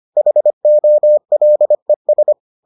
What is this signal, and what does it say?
Morse code, heard as a keyed tone, about 25 wpm, 600 Hz. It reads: HOLES